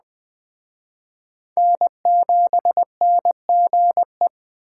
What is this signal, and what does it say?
Morse code, heard as a keyed tone, about 21 wpm, 700 Hz. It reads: N7NGE